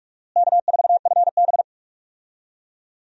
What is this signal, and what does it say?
Morse code, heard as a keyed tone, about 46 wpm, 700 Hz. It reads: K4FB
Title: USA Call Signs - 4 Characters - Repeated 3x - 45wpm